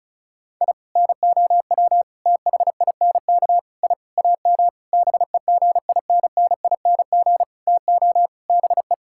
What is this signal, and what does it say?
Morse code, heard as a keyed tone, about 34 wpm, 700 Hz. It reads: I NOW THINK I AM BEGINNING TO BE
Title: Sentences from Top 1000 Words - Review Part 5 - 35wpm